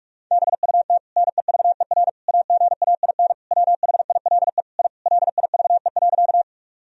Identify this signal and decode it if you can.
Morse code, heard as a keyed tone, about 48 wpm, 700 Hz. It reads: BUT NEVER AGAIN WHILE I LIVE.
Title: Sentences from Top 1000 Words - Review Part 1 - 45wpm